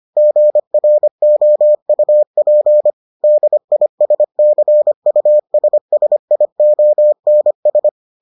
Morse code, heard as a keyed tone, about 25 wpm, 600 Hz. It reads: GROUP DISCUSSIONS